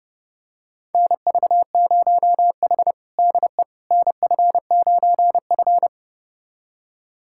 Morse code, heard as a keyed tone, about 30 wpm, 700 Hz. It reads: NV0H DE NF9F
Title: USA Callsign DE Callsign - 4 Characters - 30wpm